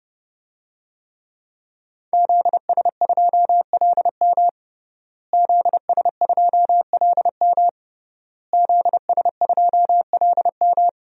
Morse code, heard as a keyed tone, about 30 wpm, 700 Hz. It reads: ZS2LM ZS2LM ZS2LM
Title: DXCC-100 Call Signs - 5 Characters - Repeated 3x - 30wpm